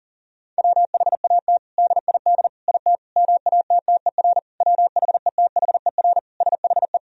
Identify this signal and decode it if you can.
Morse code, heard as a keyed tone, about 40 wpm, 700 Hz. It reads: WHAT DID IT MATTER WHETHER SHE
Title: Sentences from Top 1000 Words - Review Part 3 - 40wpm